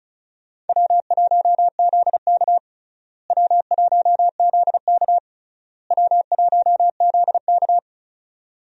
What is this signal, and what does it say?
Morse code, heard as a keyed tone, about 36 wpm, 700 Hz. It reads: W1ZK W1ZK W1ZK
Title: USA Call Signs - 4 Characters - Repeated 3x - 35wpm